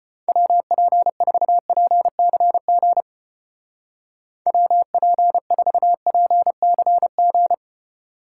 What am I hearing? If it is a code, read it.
Morse code, heard as a keyed tone, about 31 wpm, 700 Hz. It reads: WP4PCG WP4PCG
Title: Speed Racing - USA 2x3 Call Signs 25wpm